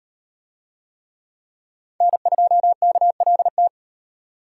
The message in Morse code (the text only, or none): N2KLT